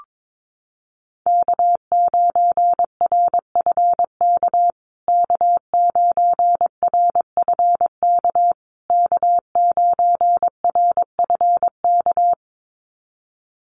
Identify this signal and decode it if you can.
Morse code, heard as a keyed tone, about 22 wpm, 700 Hz. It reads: K9RFK K9RFK K9RFK
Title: USA Call Signs Repeated 3x 22wpm